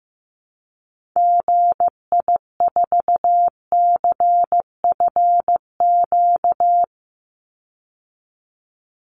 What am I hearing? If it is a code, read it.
Morse code, heard as a keyed tone, about 15 wpm, 700 Hz. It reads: GI4CFQ
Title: Call Signs Repeated 3x - 15wpm